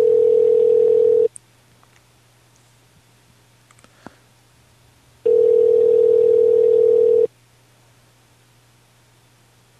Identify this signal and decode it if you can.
Ringback tone (North America), 440 + 480 Hz.